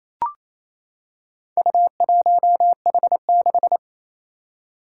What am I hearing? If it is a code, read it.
Morse code, heard as a keyed tone, about 28 wpm, 700 Hz. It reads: U1H6
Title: Four Challenging Letter-Number Combinations 28wpm